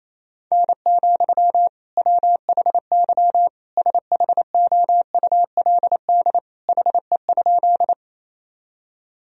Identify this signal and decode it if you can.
Morse code, heard as a keyed tone, about 28 wpm, 700 Hz. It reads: N, WHY SHOULD HE?
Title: Sentences from Top 300 Words and Abbreviations 28wpm